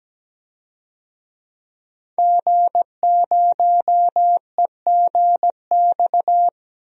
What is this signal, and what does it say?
Morse code, heard as a keyed tone, about 17 wpm, 700 Hz. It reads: G0EGX
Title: DXCC-100 Call Signs - 5 Characters - 17wpm